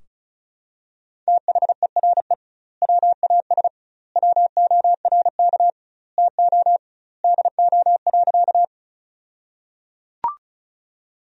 Morse code, heard as a keyed tone, about 36 wpm, 700 Hz. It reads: THERE WAS WORK TO DO.
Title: 2x Word-Spacing - Sentences from Top 200 Words - 35wpm